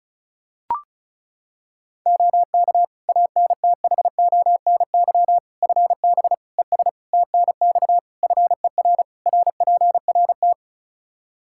Morse code, heard as a keyed tone, about 36 wpm, 700 Hz. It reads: OK ANTHONY FB ES TNX FER RPRT
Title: Practice QSO - Exchange 2 - Info - 35wpm